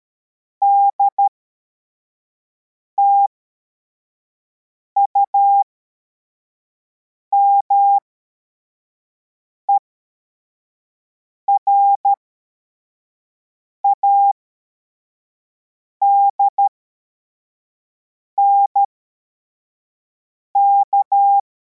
Morse code, heard as a keyed tone, about 13 wpm, spaced as for 4 wpm, 800 Hz. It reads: DTUME RADNK
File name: formation / cw / Text41